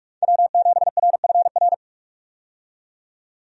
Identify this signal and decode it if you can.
Morse code, heard as a keyed tone, about 46 wpm, 700 Hz. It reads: W7RFR
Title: USA Call Signs - 5 Characters - 45wpm